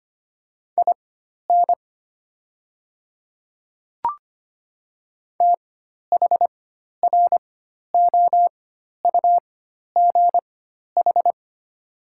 Morse code, heard as a keyed tone, about 25 wpm, spaced as for 12 wpm, 700 Hz. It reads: IN THROUGH
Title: 4x Character Spacing - Top 200 Words - 25wpm